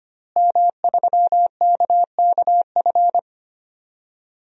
Morse code, heard as a keyed tone, about 25 wpm, 700 Hz. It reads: M3KKF
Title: DXCC-100 Call Signs - 5 Characters - Repeated 3x - 25wpm